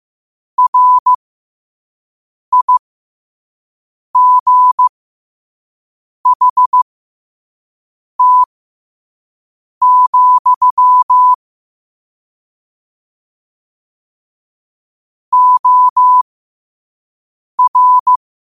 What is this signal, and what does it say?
Morse code, heard as a keyed tone, about 15 wpm, spaced as for 5 wpm, 1000 Hz. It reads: RIGHT, OR
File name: qotdfiles / 5wpm